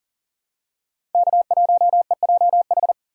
Morse code, heard as a keyed tone, about 38 wpm, 700 Hz. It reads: K1EJH
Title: USA Call Signs - 5 Characters - 6x Character Spacing - 40wpm